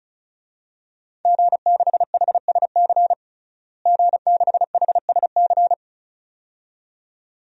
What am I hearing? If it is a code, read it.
Morse code, heard as a keyed tone, about 34 wpm, 700 Hz. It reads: G6HSC G6HSC
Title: DXCC-100 Call Signs - 5 Characters - Repeated 2x - 35wpm